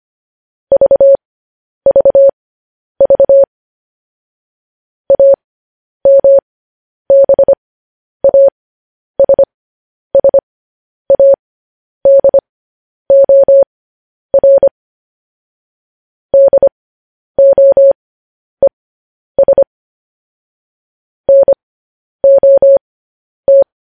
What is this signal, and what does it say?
Morse code, heard as a keyed tone, about 25 wpm, spaced as for 10 wpm, 550 Hz. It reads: VVV AMBASSADOR DOES NOT
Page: news headlines in morse code at 10 wpm